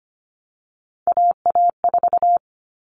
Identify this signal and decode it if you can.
Morse code, heard as a keyed tone, about 25 wpm, 700 Hz. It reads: AA4
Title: All Cut Numbers - 3-Digit 25wpm